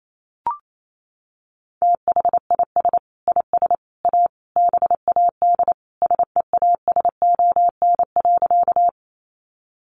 Morse code, heard as a keyed tone, about 28 wpm, 700 Hz. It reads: THIS IS A BAD SEASON.
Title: Sentences from Top 1400 Words 28wpm